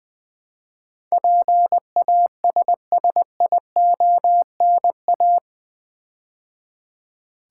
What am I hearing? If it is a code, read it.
Morse code, heard as a keyed tone, about 20 wpm, 700 Hz. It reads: PASSIONA
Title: Build A Word - 4 Letters At A Time - 10-Letter Words - 20wpm